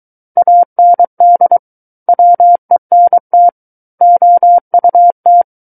Morse code, heard as a keyed tone, about 23 wpm, 700 Hz. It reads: AND WENT OUT
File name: Book One / 14-Chapter 14 - 23 WPM